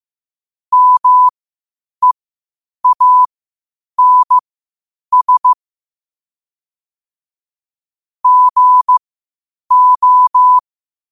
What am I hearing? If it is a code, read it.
Morse code, heard as a keyed tone, about 15 wpm, spaced as for 8 wpm, 1000 Hz. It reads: MEANS GO
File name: qotdfiles / 8wpm